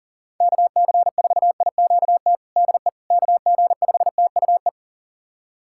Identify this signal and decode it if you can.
Morse code, heard as a keyed tone, about 40 wpm, 700 Hz. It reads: KC4IQT DE KG5TUE
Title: USA Callsign DE Callsign 40wpm